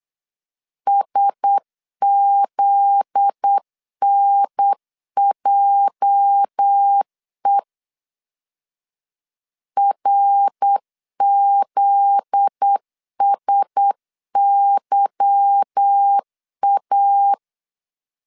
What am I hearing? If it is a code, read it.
Morse code, heard as a keyed tone, about 8 wpm, 800 Hz. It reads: SZNJE RZSYA
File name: LEZIONI / 8 YJ45 / 8 YJ 8wpm 3313